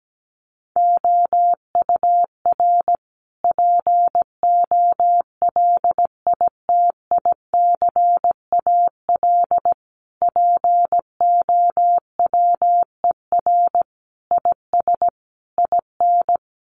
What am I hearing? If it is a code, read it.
Morse code, heard as a keyed tone, about 17 wpm, 700 Hz. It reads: OUR POLITICAL POWER IS IN
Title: Sentences from Top 900 Words 17wpm